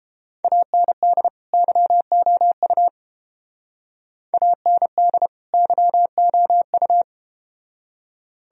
Morse code, heard as a keyed tone, about 32 wpm, 700 Hz. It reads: AND YOU AND YOU